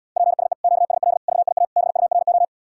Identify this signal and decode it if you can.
Morse code, heard as a keyed tone, about 80 wpm, 700 Hz. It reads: PSE QSL VIA BURO